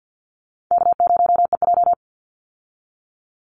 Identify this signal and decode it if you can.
Morse code, heard as a keyed tone, about 50 wpm, 700 Hz. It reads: K0EJ
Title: Easy Call Signs Repeated 2x - 50wpm